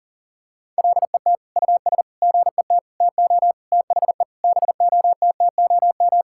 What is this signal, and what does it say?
Morse code, heard as a keyed tone, about 40 wpm, 700 Hz. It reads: LET US GET TO THE BOTTOM
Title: Sentences from Top 1000 Words - Review Part 6 - 40wpm